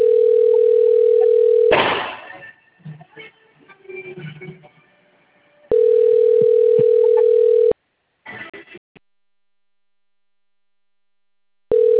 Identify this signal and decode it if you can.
Ringback tone (North America), 440 + 480 Hz.
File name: rg-1600-unknown- - - .6368